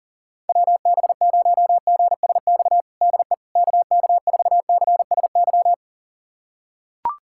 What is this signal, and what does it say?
Morse code, heard as a keyed tone, about 40 wpm, 700 Hz. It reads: WB0GSX DE KK4CSY
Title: USA Callsign DE Callsign - 6 Characters - 40wpm